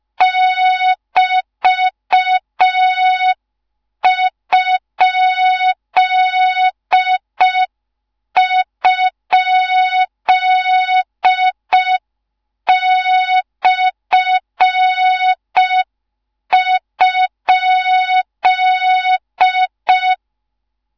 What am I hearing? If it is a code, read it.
Morse code, heard as a keyed tone, about 5 wpm, 3700 Hz. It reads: <BT>??/?
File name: Morze-HA2MN / hang / 13IRASJELEK1 HA2MN